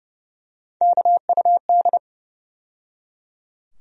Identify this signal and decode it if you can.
Morse code, heard as a keyed tone, about 30 wpm, 700 Hz. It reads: KUD
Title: USA Call Sign Postfixes - 3 Letters 30wpm